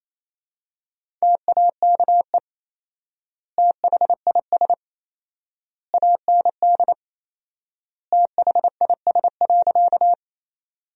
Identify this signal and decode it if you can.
Morse code, heard as a keyed tone, about 28 wpm, 700 Hz. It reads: TAKE THIS AND THIS.
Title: 4x Word-Spacing - Sentences from Top 200 Words - 28wpm